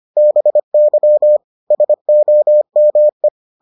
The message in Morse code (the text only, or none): BY SOME